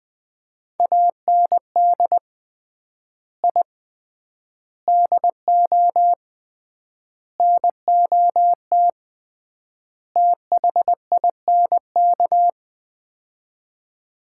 Morse code, heard as a keyed tone, about 20 wpm, 700 Hz. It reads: AND I DO NOT THINK
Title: 3x Word-Spacing - Sets of 5 Words - 20wpm